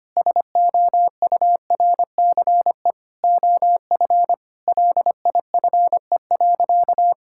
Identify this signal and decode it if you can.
Morse code, heard as a keyed tone, about 25 wpm, 700 Hz. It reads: SOURCE OF LIFE.
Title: Sentences from Top 1000 Words - Review Part 5 - 25wpm